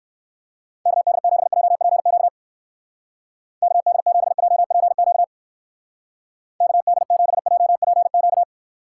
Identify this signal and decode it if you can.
Morse code, heard as a keyed tone, about 57 wpm, 700 Hz. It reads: KD7JPQ KD7JPQ KD7JPQ